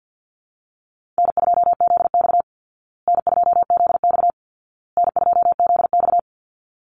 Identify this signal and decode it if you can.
Morse code, heard as a keyed tone, about 48 wpm, 700 Hz. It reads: N2ZX N2ZX N2ZX
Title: Easy Call Signs Repeated 3x - 50wpm